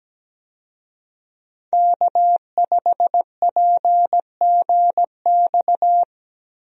Morse code, heard as a keyed tone, about 17 wpm, 700 Hz. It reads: K5PGX
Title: USA Call Signs - 5 Characters - Repeated 2x - 17wpm